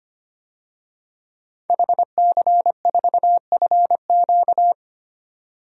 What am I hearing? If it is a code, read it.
Morse code, heard as a keyed tone, about 25 wpm, 700 Hz. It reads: HC4FQ
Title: DXCC-100 Call Signs - 5 Characters - 25wpm